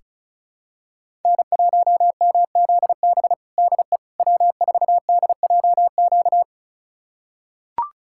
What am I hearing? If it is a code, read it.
Morse code, heard as a keyed tone, about 36 wpm, 700 Hz. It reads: N1MZB DE W4DJQ
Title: USA Callsign DE Callsign - 5 Characters - 35wpm